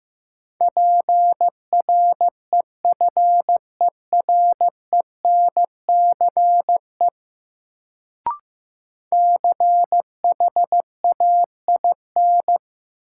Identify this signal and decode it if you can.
Morse code, heard as a keyed tone, about 15 wpm, 700 Hz. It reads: PREFERENCE CHAIN
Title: Sentences from Top 1500 Words - New Word Drill Alt - 15wpm